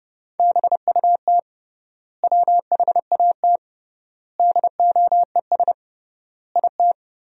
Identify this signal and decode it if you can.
Morse code, heard as a keyed tone, about 31 wpm, 700 Hz. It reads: BUT WHAT DOES IT
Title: 3x Word-Spacing - Sentences from Top 200 Words - 30wpm